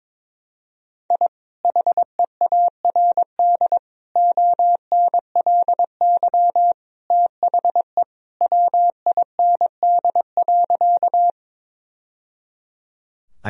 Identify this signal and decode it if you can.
Morse code, heard as a keyed tone, about 21 wpm, 700 Hz. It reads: I HEARD ONLY THE WIND.
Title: Sentences from Top 1000 Words - Review Part 2 - 22wpm